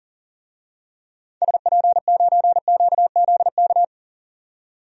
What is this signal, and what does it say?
Morse code, heard as a keyed tone, about 40 wpm, 700 Hz. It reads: SP9QZK
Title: DXCC-100 Call Signs - 6 Characters - 40wpm